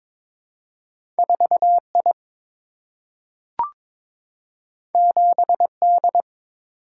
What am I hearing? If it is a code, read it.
Morse code, heard as a keyed tone, about 22 wpm, 700 Hz. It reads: 4 I 7 D
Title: Single-Letter Single-Number 22wpm